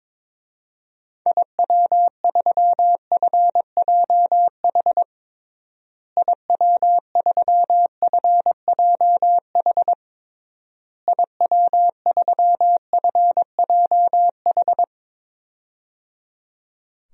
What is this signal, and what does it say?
Morse code, heard as a keyed tone, about 22 wpm, 700 Hz. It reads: IW3FJH IW3FJH IW3FJH
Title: DXCC-100 Call Signs - 6 Characters - Repeated 3x - 22wpm